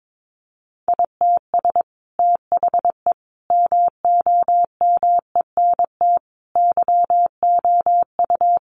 Morse code, heard as a keyed tone, about 22 wpm, 700 Hz. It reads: ITS THE MOMENT YOU
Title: Sentences from Top 1000 Words 22wpm